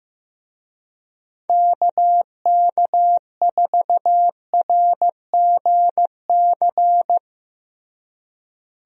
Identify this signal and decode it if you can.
Morse code, heard as a keyed tone, about 15 wpm, 700 Hz. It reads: KK4RGC
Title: USA Call Signs - 6 Characters - Repeated 3x - 15wpm